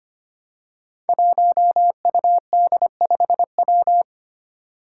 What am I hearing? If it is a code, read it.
Morse code, heard as a keyed tone, about 25 wpm, 700 Hz. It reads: 1UD5W
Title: Five Challenging Letter-Number Combinations 25wpm